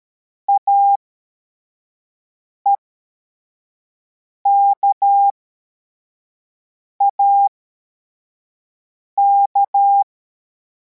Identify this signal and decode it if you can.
Morse code, heard as a keyed tone, about 13 wpm, spaced as for 4 wpm, 800 Hz. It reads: AEKAK